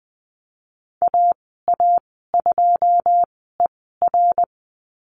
Morse code, heard as a keyed tone, about 20 wpm, spaced as for 14 wpm, 700 Hz. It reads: AA2ER
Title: 140 Lesson 23g - Letter f - Call Signs 20wpm